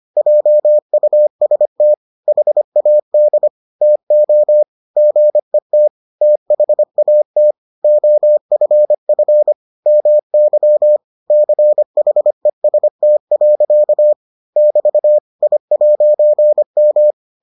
Morse code, heard as a keyed tone, about 25 wpm, 600 Hz. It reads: JUST HAD TO GET THAT OFF MY CHEST. <BT> I'M